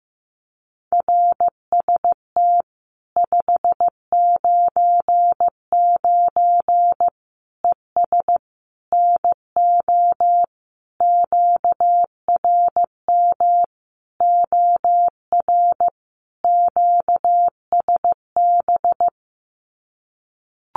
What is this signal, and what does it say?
Morse code, heard as a keyed tone, about 15 wpm, 700 Hz. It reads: RST 599 ES NO QRM OR QSB
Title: Common QSO Phrases 15wpm